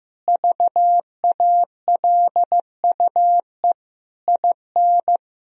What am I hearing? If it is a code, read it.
Morse code, heard as a keyed tone, about 16 wpm, 700 Hz. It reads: VALUE IN
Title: Sentences from Top 1000 Words - Review Part 3 - 15wpm